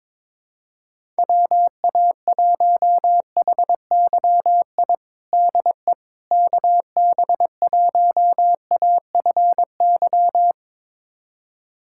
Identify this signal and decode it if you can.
Morse code, heard as a keyed tone, about 22 wpm, 700 Hz. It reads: WA1HYI DE KB1AFY
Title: USA Callsign DE Callsign - 6 Characters - 22wpm